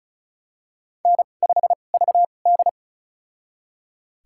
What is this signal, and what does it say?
Morse code, heard as a keyed tone, about 36 wpm, spaced as for 25 wpm, 700 Hz. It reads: N5VD
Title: USA Call Signs - 2x Character Spacing - 35wpm